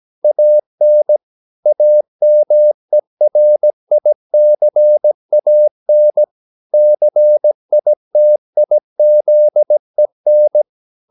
Morse code, heard as a keyed tone, about 17 wpm, 600 Hz. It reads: AN AMERICAN CITIZEN
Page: Romanen Never af Ken Follet på engelsk.